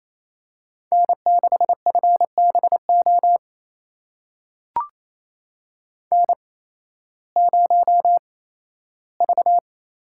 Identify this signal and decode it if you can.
Morse code, heard as a keyed tone, about 28 wpm, spaced as for 15 wpm, 700 Hz. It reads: N6FBO N0V
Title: USA Call Signs - 5 Characters - 8x Character Spacing - 28wpm